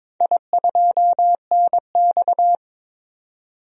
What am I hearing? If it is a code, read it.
Morse code, heard as a keyed tone, about 22 wpm, 700 Hz. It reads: I2NX
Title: DXCC-100 Call Signs - 4 Characters - 22wpm